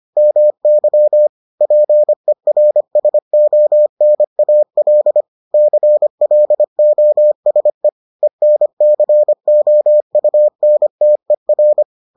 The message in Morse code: MY PERSONAL CLOSE ENCOUNTER

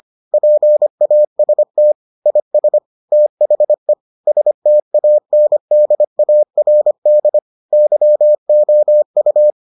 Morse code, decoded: PAST IS THE STANDARD YOU

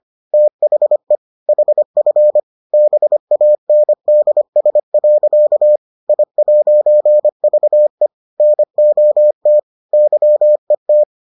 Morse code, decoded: THE HF BANDS. I'VE NOT YET

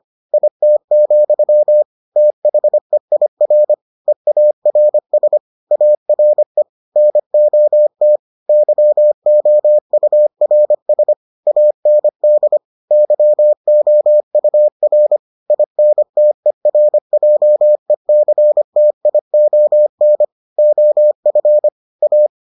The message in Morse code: IT, THEIR EARS ARE NOT YOURS AND YOUR INTERJECTION OF A

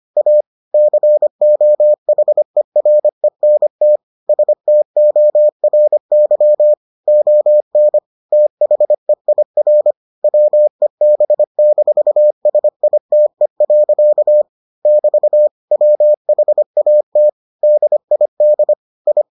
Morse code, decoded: A COHERENT STORY ON THEIR WEB-SITE. <BT> WHAT DID I

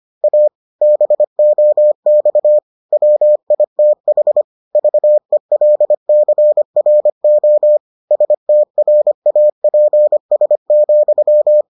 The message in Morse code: A BOX WITH VELCRO STRAPS,

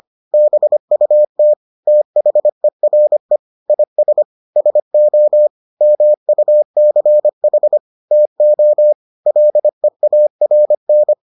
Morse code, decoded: BUT THERE IS SO MUCH TO LEARN

